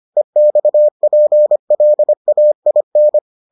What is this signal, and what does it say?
Morse code, heard as a keyed tone, about 25 wpm, 600 Hz. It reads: EXPLAIN